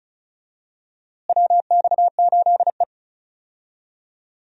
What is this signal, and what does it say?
Morse code, heard as a keyed tone, about 36 wpm, 700 Hz. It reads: WX8E
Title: USA Call Signs - 4 Characters - Repeated 2x - 35wpm